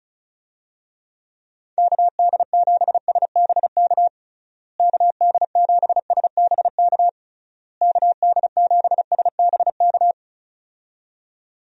Morse code, heard as a keyed tone, about 34 wpm, 700 Hz. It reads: KD7SBK KD7SBK KD7SBK
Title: USA Call Signs - 6 Characters - Repeated 3x - 35wpm